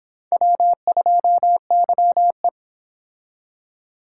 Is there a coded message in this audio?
Yes: Morse code from 0.3 to 2.5 s, heard as a keyed tone, about 29 wpm, 700 Hz. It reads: W2YE